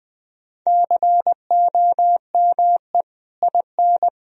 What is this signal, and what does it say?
Morse code, heard as a keyed tone, about 19 wpm, 700 Hz. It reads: COME IN
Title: Sentences from Top 1000 Words - Review Part 4 - 20wpm